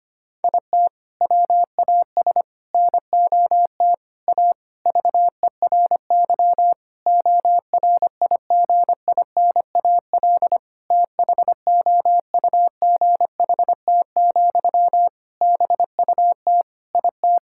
Morse code, heard as a keyed tone, about 25 wpm, 700 Hz. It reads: IT WAS NOT A VERY ORIGINAL THOUGHT, BUT IT
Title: Sentences from Top 1000 Words - Review Part 6 - 25wpm